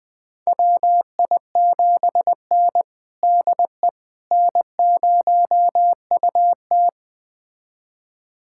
Morse code, heard as a keyed tone, about 20 wpm, 700 Hz. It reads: WI7N DE N0UT
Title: USA Callsign DE Callsign - 4 Characters - 20wpm